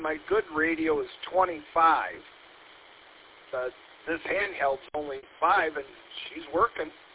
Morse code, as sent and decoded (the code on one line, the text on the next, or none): none